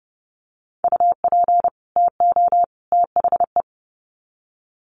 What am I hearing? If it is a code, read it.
Morse code, heard as a keyed tone, about 30 wpm, 700 Hz. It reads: UP TO THE
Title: Sets of 3 Words 30wpm